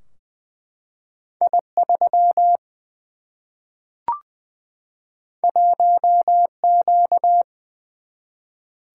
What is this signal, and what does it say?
Morse code, heard as a keyed tone, about 20 wpm, 700 Hz. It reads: I 3 1 Q
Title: Single-Letter Single-Number 20wpm